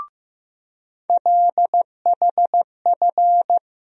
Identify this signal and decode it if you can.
Morse code, heard as a keyed tone, about 15 wpm, 700 Hz. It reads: LHF
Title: USA Call Sign Postfixes - 3 Letters 15wpm